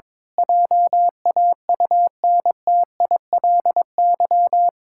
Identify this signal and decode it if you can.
Morse code, heard as a keyed tone, about 22 wpm, 700 Hz. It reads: JAUNTILY
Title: Pangram Challenge v2 - 22wpm